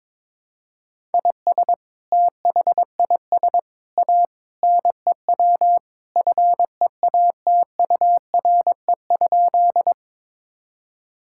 Morse code, heard as a keyed tone, about 22 wpm, 700 Hz. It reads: IS THIS A NEW FEATURE?
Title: Sentences from Top 1000 Words - Review Part 5 - 22wpm